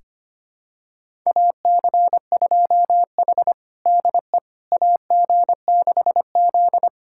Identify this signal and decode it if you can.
Morse code, heard as a keyed tone, about 25 wpm, 700 Hz. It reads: AC2H DE AG6Z
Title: USA Callsign DE Callsign - 4 Characters - 25wpm